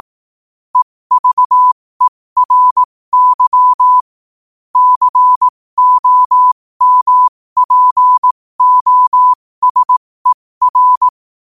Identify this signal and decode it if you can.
Morse code, heard as a keyed tone, about 18 wpm, spaced as for 15 wpm, 1000 Hz. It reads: EVERY COMPOSER
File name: qotdfiles / 15wpm